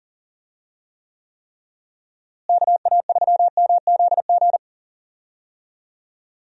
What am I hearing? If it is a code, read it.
Morse code, heard as a keyed tone, about 37 wpm, 700 Hz. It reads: KA3MZG
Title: USA Call Signs - 6 Characters - Repeated 3x - 40wpm